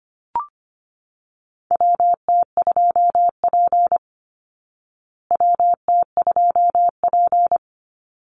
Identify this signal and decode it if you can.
Morse code, heard as a keyed tone, about 24 wpm, 700 Hz. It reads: WT2P WT2P
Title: Call Signs Repeated 2x - 25wpm